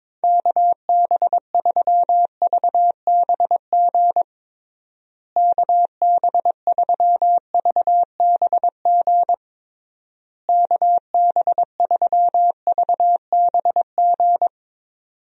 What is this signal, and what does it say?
Morse code, heard as a keyed tone, about 22 wpm, 700 Hz. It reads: KB3VBG KB3VBG KB3VBG
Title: USA Call Signs - 6 Characters - Repeated 3x - 22wpm